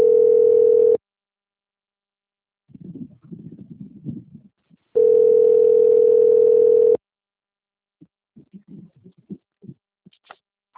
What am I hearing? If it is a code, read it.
Ringback tone (North America), 440 + 480 Hz.